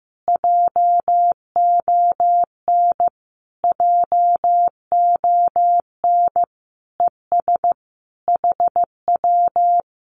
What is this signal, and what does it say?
Morse code, heard as a keyed tone, about 15 wpm, 700 Hz. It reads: JON JON ES HW
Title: Common QSO Phrases Encore 15wpm